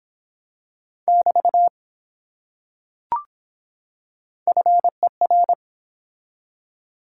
Morse code, heard as a keyed tone, about 23 wpm, 700 Hz. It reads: <BT> FER